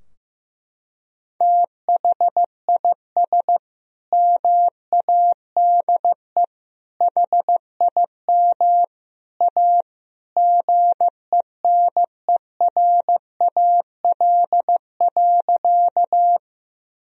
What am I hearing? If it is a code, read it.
Morse code, heard as a keyed tone, about 14 wpm, 700 Hz. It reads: THIS MADE HIM A GENERAL.
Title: Sentences from Top 1000 Words - Review Part 3 - 15wpm